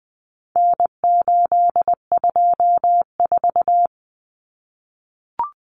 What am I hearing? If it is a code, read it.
Morse code, heard as a keyed tone, about 20 wpm, 700 Hz. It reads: N824
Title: Common Cut Numbers - 4-Digit 20wpm